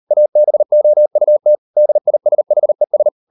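Morse code, heard as a keyed tone, about 38 wpm, 600 Hz. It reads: ABOUT DISHES
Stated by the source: Morse code practice at 38 WPM, sending Ken Follett's novel Never in English, from CW med Gnister.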